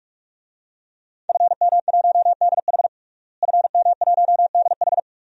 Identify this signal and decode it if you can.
Morse code, heard as a keyed tone, about 46 wpm, 700 Hz. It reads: FM1DH FM1DH
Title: DXCC-100 Call Signs - 5 Characters - Repeated 2x - 45wpm